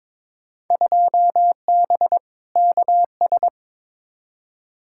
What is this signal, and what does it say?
Morse code, heard as a keyed tone, about 22 wpm, 700 Hz. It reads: 2B KS
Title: Field Day Exchanges Encore - 22wpm